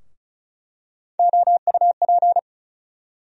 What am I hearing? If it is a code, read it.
Morse code, heard as a keyed tone, about 34 wpm, 700 Hz. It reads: OUP
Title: USA Call Sign Postfixes - 3 Letters 35wpm